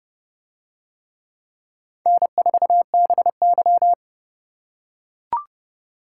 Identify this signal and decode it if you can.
Morse code, heard as a keyed tone, about 28 wpm, 700 Hz. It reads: N4BY